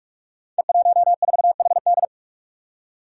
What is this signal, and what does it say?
Morse code, heard as a keyed tone, about 44 wpm, 700 Hz. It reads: E14HD
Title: Five Challenging Letter-Number Combinations 45wpm